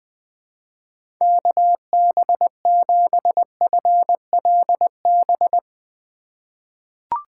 Morse code, heard as a keyed tone, about 21 wpm, 700 Hz. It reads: KB7FLB
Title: USA Call Signs - 6 Characters - 3x Character Spacing - 20wpm